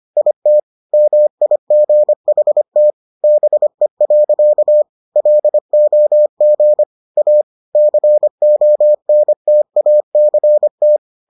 Morse code, heard as a keyed tone, about 25 wpm, 600 Hz. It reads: IT MIGHT BE. LOG A CONTACT